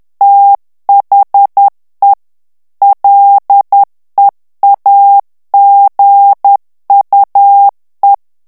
Morse code, heard as a keyed tone, about 10 wpm, 800 Hz. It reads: THE LEAGUE
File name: ANGLAIS texte clair 3 vitesse 10 mots minute